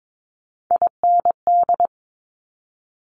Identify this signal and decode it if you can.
Morse code, heard as a keyed tone, about 22 wpm, 700 Hz. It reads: IND
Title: Top Three Letter Combinations in Words 22wpm